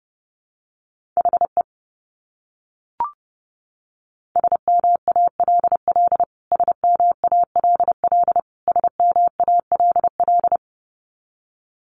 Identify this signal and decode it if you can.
Morse code, heard as a keyed tone, about 30 wpm, 700 Hz. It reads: HE SMALL SMALL SMALL
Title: Top 300 Words Repeated 30wpm